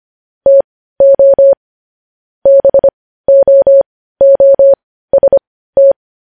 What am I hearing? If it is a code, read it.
Morse code, heard as a keyed tone, about 25 wpm, spaced as for 15 wpm, 550 Hz. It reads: TO BOOST
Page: news-headlines-in-morse-code-at-15-wpm